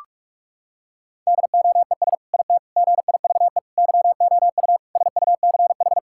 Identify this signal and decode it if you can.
Morse code, heard as a keyed tone, about 46 wpm, 700 Hz. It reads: DOES IT GIVE YOU SUCH
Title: Sentences from Top 1600 Words 45wpm